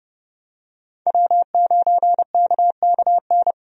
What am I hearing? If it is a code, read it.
Morse code, heard as a keyed tone, about 30 wpm, 700 Hz. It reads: W9KKN
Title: USA Call Signs - 5 Characters - 30wpm